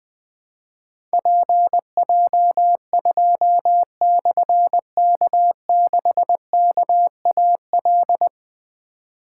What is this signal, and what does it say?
Morse code, heard as a keyed tone, about 20 wpm, 700 Hz. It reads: PJ2/K6KAL
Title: Difficult Call Signs 20wpm